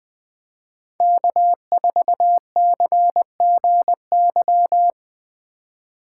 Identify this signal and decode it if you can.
Morse code, heard as a keyed tone, about 20 wpm, 700 Hz. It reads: K4CGY
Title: 2023 POTA Hunter Call Signs - 20wpm